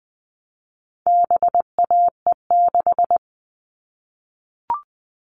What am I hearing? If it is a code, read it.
Morse code, heard as a keyed tone, about 20 wpm, 700 Hz. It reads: BAE6
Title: All Cut Numbers - 4-Digit 20wpm